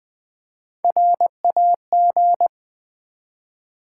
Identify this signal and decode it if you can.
Morse code, heard as a keyed tone, about 20 wpm, 700 Hz. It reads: RAG
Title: Build A Word - 1 Letter At A Time - 4-Letter Words - 20wpm